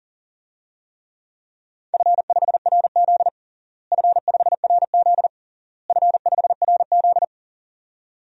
Morse code, heard as a keyed tone, about 40 wpm, 700 Hz. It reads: F5RZ F5RZ F5RZ
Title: DXCC-100 Call Signs - 4 Characters - Repeated 3x - 40wpm